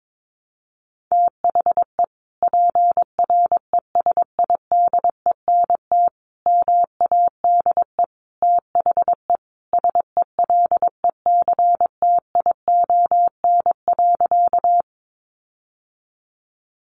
Morse code, heard as a keyed tone, about 22 wpm, 700 Hz. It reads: THE PRESIDENT MADE THE SELECTION.
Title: Sentences from Top 1400 Words 22wpm